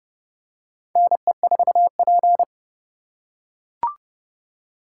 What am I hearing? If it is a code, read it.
Morse code, heard as a keyed tone, about 30 wpm, 700 Hz. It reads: NE4P
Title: USA Call Signs - 4x Character Spacing - 30wpm